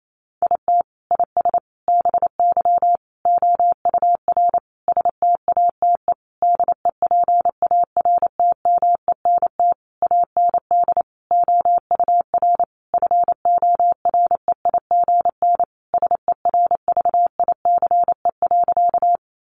Morse code, heard as a keyed tone, about 28 wpm, 700 Hz. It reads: IT IS BY OUR STATE DEPARTMENT AND OUR FOREIGN SERVICE.
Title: Sentences from Top 1400 Words 28wpm